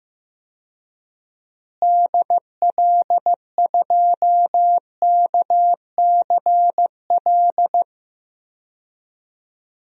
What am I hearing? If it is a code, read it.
Morse code, heard as a keyed tone, about 15 wpm, 700 Hz. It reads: DL2KCL
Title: DXCC-100 Call Signs - 6 Characters - Repeated 3x - 15wpm